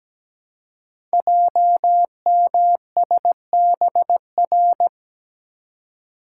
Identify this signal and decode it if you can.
Morse code, heard as a keyed tone, about 17 wpm, 700 Hz. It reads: JMSBR
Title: Code Groups - 5 Letters - 17wpm